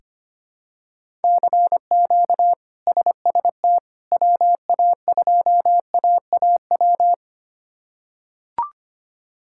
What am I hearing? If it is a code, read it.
Morse code, heard as a keyed tone, about 25 wpm, 700 Hz. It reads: CQ SST WA2AAW
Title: K1USN SST QSO Practice - 25wpm